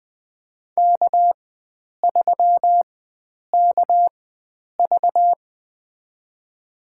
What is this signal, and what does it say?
Morse code, heard as a keyed tone, about 21 wpm, spaced as for 9 wpm, 700 Hz. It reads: K3KV
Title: USA Call Signs - 4 Characters - 4x Character Spacing - 20wpm